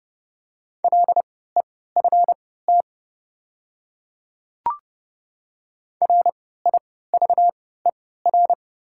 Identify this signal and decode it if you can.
Morse code, heard as a keyed tone, about 30 wpm, spaced as for 17 wpm, 700 Hz. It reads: LEFT RIVER
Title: 3x Character Spacing - Top 500 Words - 30wpm